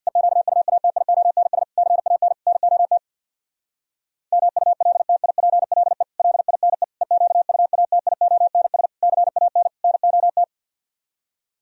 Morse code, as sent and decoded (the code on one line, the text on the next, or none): . --.- ..- .- - .. --- -. ... / -.-. .- -. / -. --- - / -- ..- .-.. - .. .--. .-.. / .-.. .. -. / . --.- ..- .- - .. --- -. ... / -.-. .- -. / -. --- -
EQUATIONS CAN NOT MULTIPLE LINE EQUATIONS CAN NOT